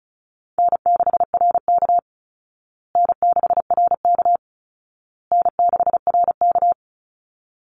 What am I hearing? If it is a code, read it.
Morse code, heard as a keyed tone, about 34 wpm, 700 Hz. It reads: N6RK N6RK N6RK
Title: Easy Call Signs Repeated 3x - 35wpm